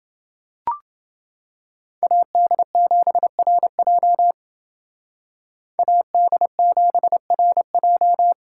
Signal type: Morse code

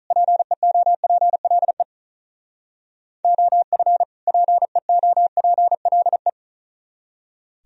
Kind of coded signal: Morse code

